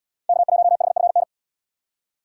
Morse code, heard as a keyed tone, about 72 wpm, 700 Hz. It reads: K1HFN